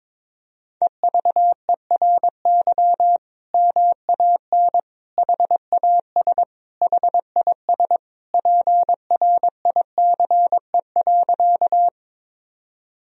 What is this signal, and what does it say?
Morse code, heard as a keyed tone, about 22 wpm, 700 Hz. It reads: EVERY MAN HAS HIS PRICE.
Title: Sentences from Top 1000 Words - Review Part 4 - 22wpm